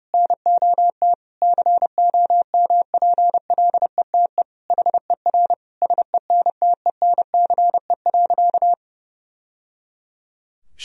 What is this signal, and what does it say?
Morse code, heard as a keyed tone, about 30 wpm, 700 Hz. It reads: NOT COMPLETE HER SENTENCE.
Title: Sentences from Top 1000 Words - Review Part 2 - 30wpm